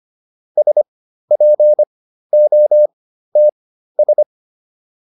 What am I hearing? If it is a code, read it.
Morse code, heard as a keyed tone, about 25 wpm, spaced as for 13 wpm, 600 Hz. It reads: SPOTS